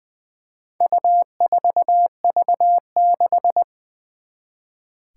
Morse code, heard as a keyed tone, about 21 wpm, 700 Hz. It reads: U4V6
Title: Four Challenging Letter-Number Combinations 20wpm